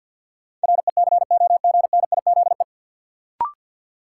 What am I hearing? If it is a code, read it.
Morse code, heard as a keyed tone, about 48 wpm, 700 Hz. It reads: RECOGNIZE
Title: Sentences from Top 1400 Words - New Word Drill Alt - 50wpm